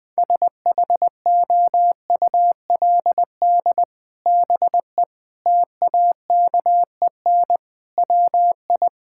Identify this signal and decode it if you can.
Morse code, heard as a keyed tone, about 21 wpm, 700 Hz. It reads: SHOULD BE TAKEN WI
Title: Sets of 8 Words - 20wpm